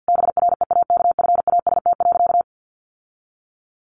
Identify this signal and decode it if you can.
Morse code, heard as a keyed tone, about 50 wpm, 700 Hz. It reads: BREAKFAST.
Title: Sentences from Top 1500 Words 50wpm